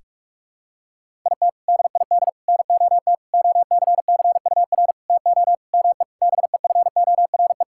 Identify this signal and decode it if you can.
Morse code, heard as a keyed tone, about 46 wpm, 700 Hz. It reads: IT DID NOT OCCUR TO ME BEFORE
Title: Sentences from Top 1000 Words - Review Part 6 - 45wpm